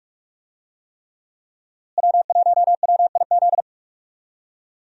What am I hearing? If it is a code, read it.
Morse code, heard as a keyed tone, about 46 wpm, 700 Hz. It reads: W1WIZ
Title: USA Call Signs - 5 Characters - Repeated 3x - 45wpm